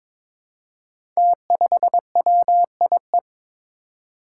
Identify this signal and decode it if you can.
Morse code, heard as a keyed tone, about 23 wpm, 700 Hz. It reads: T5WIE
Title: Five Challenging Letter-Number Combinations 22wpm